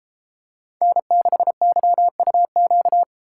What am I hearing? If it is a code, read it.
Morse code, heard as a keyed tone, about 33 wpm, 700 Hz. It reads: N6YUQ